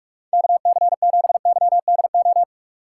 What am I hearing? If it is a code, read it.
Morse code, heard as a keyed tone, about 61 wpm, spaced as for 53 wpm, 700 Hz. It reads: KC7YDO
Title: Speed Racing - USA 2x3 Call Signs 45wpm